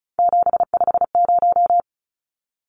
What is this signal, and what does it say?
Morse code, heard as a keyed tone, about 34 wpm, 700 Hz. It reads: 750
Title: Numbers - 3-Digit 35wpm